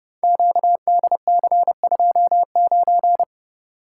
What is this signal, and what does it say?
Morse code, heard as a keyed tone, about 30 wpm, 700 Hz. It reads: QDC29
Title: Code Groups - 5 Letter:Numbers - 30wpm